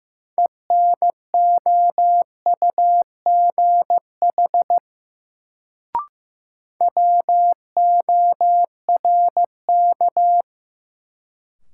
Morse code, heard as a keyed tone, about 15 wpm, 700 Hz. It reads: ENOUGH WORK
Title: Sets of 2 Words - New Word Drill - 15wpm